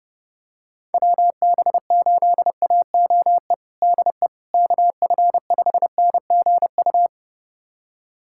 Morse code, heard as a keyed tone, about 30 wpm, 700 Hz. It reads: WB8AOE DE KF5NGU
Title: USA Callsign DE Callsign - 6 Characters - 30wpm